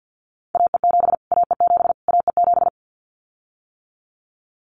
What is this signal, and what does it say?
Morse code, heard as a keyed tone, about 50 wpm, 700 Hz. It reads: AE7 AE7 AE7
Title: Callsign Prefixes for Top 25 DXCC - Repeated 3x - 50wpm